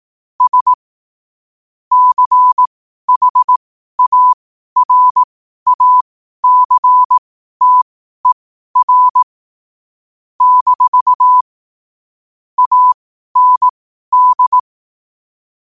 Morse code, heard as a keyed tone, about 18 wpm, spaced as for 12 wpm, 1000 Hz. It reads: S CHARACTER - AND
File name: qotdfiles / 12wpm